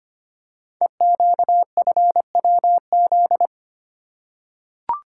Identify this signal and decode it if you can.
Morse code, heard as a keyed tone, about 25 wpm, 700 Hz. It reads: EQFWZ
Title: Code Groups - 5 Letter:Numbers - 25wpm